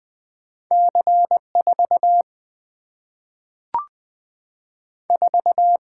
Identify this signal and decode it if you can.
Morse code, heard as a keyed tone, about 20 wpm, 700 Hz. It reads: C4 4
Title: Single-Letter Single-Number 20wpm